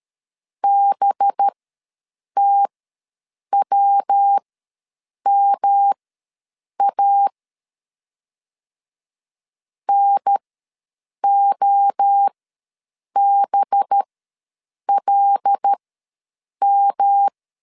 Morse code, heard as a keyed tone, about 13 wpm, spaced as for 7 wpm, 800 Hz. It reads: BTWMA NOBLM
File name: 5 LB ws 12wpm